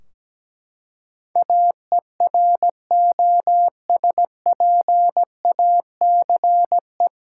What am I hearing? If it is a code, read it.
Morse code, heard as a keyed tone, about 17 wpm, 700 Hz. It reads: AEROSPACE
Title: Practice QSO - Exchange 3 - More Info - 17wpm